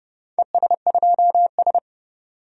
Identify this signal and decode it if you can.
Morse code, heard as a keyed tone, about 31 wpm, 700 Hz. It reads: ES2S
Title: DXCC-100 Call Signs - 4 Characters - 30wpm